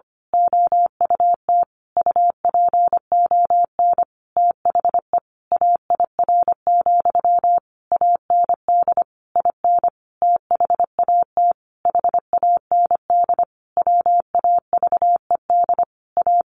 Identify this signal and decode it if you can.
Morse code, heard as a keyed tone, about 25 wpm, 700 Hz. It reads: OUT UPON THE AIR, AND IN THAT HAND WAVED A
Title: A Sound Of Thunder 25wpm